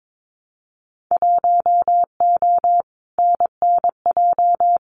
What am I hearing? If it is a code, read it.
Morse code, heard as a keyed tone, about 22 wpm, 700 Hz. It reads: 1O NNJ
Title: Winter Field Day Exchanges 22wpm